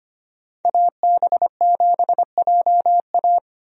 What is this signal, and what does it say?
Morse code, heard as a keyed tone, about 25 wpm, 700 Hz. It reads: AB7JA
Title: USA Call Signs - 5 Characters - 25wpm